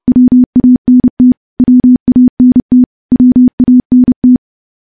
Morse code, heard as a keyed tone, about 30 wpm, 250 Hz. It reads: WANT WANT WANT
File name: morse / morse-training-30WPM